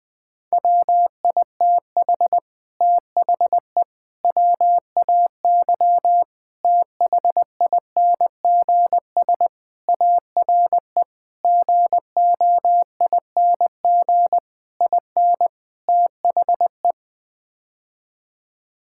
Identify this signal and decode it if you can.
Morse code, heard as a keyed tone, about 20 wpm, 700 Hz. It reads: WITH THE WAY THINGS ARE GOING IN THE
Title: Sets of 8 Words - 20wpm